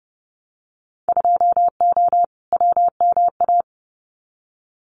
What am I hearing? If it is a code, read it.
Morse code, heard as a keyed tone, about 31 wpm, 700 Hz. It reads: 2O WMA
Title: Winter Field Day Exchanges Encore 30wpm